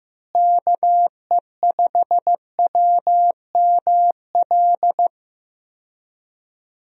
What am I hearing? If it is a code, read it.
Morse code, heard as a keyed tone, about 15 wpm, 700 Hz. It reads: KE5WML
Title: USA Call Signs - 6 Characters - Repeated 2x - 15wpm